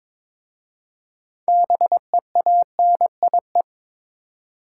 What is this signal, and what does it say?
Morse code, heard as a keyed tone, about 22 wpm, 700 Hz. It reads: BEANIE
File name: Difficult Words with Hints - 22wpm